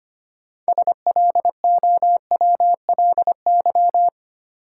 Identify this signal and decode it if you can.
Morse code, heard as a keyed tone, about 25 wpm, 700 Hz. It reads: SLOWLY
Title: Sentences from Top 1100 Words - New Word Drill - 25wpm